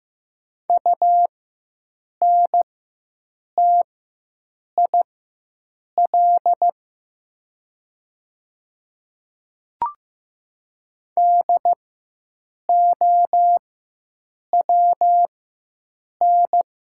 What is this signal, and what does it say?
Morse code, heard as a keyed tone, about 15 wpm, spaced as for 7 wpm, 700 Hz. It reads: UNTIL DOWN
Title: 4x Character Spacing - Top 500 Words - 15wpm